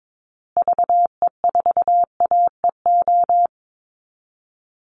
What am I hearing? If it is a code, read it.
Morse code, heard as a keyed tone, about 22 wpm, 700 Hz. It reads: VE4AEO
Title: Call Signs Repeated 2x - 22wpm